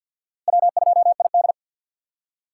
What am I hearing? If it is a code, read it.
Morse code, heard as a keyed tone, about 47 wpm, 700 Hz. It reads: W2ID
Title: K1USN SST QSO Practice - 50wpm